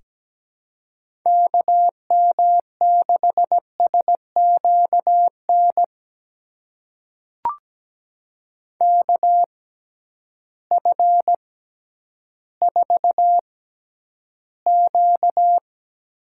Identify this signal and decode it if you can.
Morse code, heard as a keyed tone, about 17 wpm, spaced as for 10 wpm, 700 Hz. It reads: KM6SQN KF4Q
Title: USA Call Signs - 6x Character Spacing - 17wpm